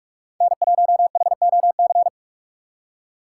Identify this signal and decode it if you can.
Morse code, heard as a keyed tone, about 47 wpm, 690 Hz. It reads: N1HOC
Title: USA Call Signs - 5 Characters - 2x Character Spacing - 45wpm